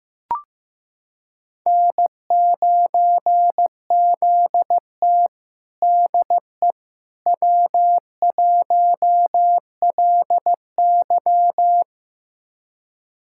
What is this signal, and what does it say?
Morse code, heard as a keyed tone, about 15 wpm, 700 Hz. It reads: N9ZT DE W1LY
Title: USA Callsign DE Callsign - 4 Characters - 15wpm